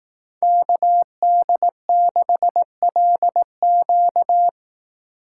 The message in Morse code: KD6LQ

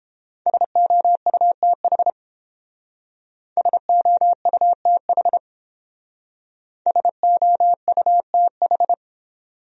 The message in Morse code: SOUTH SOUTH SOUTH

